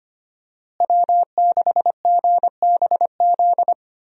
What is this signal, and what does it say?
Morse code, heard as a keyed tone, about 25 wpm, 700 Hz. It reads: W6GBZ
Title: USA Call Signs - 5 Characters - Repeated 2x - 25wpm